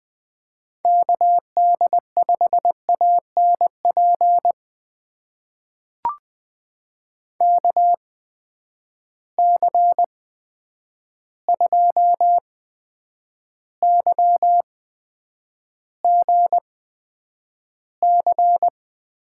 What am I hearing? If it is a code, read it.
Morse code, heard as a keyed tone, about 20 wpm, spaced as for 9 wpm, 700 Hz. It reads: KD5ANP KC2YGC
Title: USA Call Signs - 6 Characters - 8x Character Spacing - 20wpm